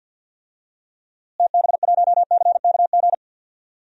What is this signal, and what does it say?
Morse code, heard as a keyed tone, about 51 wpm, 700 Hz. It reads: TB1CKG
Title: DXCC-100 Call Signs - 6 Characters - Repeated 3x - 50wpm